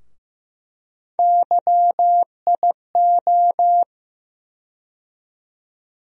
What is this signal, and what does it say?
Morse code, heard as a keyed tone, about 15 wpm, 700 Hz. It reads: YIO
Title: USA Call Sign Postfixes - 3 Letters 15wpm